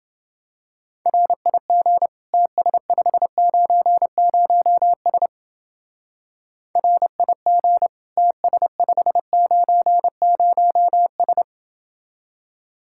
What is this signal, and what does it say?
Morse code, heard as a keyed tone, about 28 wpm, 700 Hz. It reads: RIG TS590S RIG TS590S